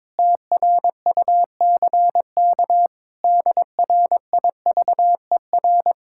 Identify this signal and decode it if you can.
Morse code, heard as a keyed tone, about 21 wpm, 700 Hz. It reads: TRUCK DRIVER
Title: Speed-Racing Common QSO Phrases - 22wpm